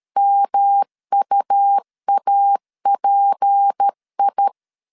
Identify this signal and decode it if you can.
Morse code, heard as a keyed tone, about 13 wpm, 800 Hz. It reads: MUAPI
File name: CW / DET / 1 9 1500C S C 12wpm 3811